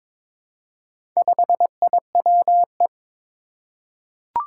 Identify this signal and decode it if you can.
Morse code, heard as a keyed tone, about 22 wpm, 700 Hz. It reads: 5IWE
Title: Four Challenging Letter-Number Combinations 22wpm